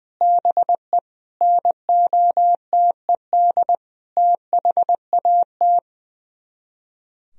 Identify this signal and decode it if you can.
Morse code, heard as a keyed tone, about 20 wpm, 700 Hz. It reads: BE NOTED THAT
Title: Speed-Racing - Sets of 5 Words 20wpm